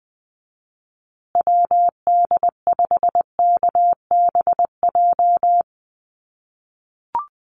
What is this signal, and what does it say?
Morse code, heard as a keyed tone, about 20 wpm, 700 Hz. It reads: WD5KBJ
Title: USA Call Signs Repeated 2x 20wpm